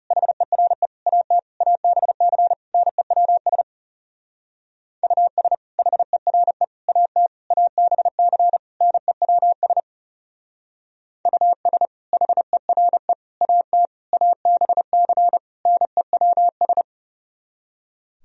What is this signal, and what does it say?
Morse code, heard as a keyed tone, about 38 wpm, 700 Hz. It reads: HERE AT ABC NEWS US HERE AT ABC NEWS US HERE AT ABC NEWS